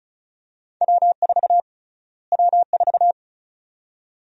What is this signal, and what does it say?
Morse code, heard as a keyed tone, about 36 wpm, 700 Hz. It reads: W4 W4
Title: USA Call Sign Prefixes - Repeated 2x - 35wpm